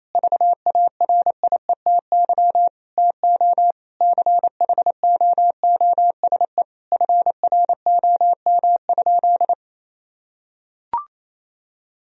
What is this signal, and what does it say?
Morse code, heard as a keyed tone, about 28 wpm, 700 Hz. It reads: VARIETY TO CHOOSE FROM?
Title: Sentences from Top 1000 Words - Review Part 4 - 28wpm